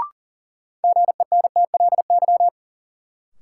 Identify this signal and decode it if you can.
Morse code, heard as a keyed tone, about 40 wpm, 700 Hz. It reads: GENTLY
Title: Sentences from Top 1500 Words - New Word Drill Alt - 40wpm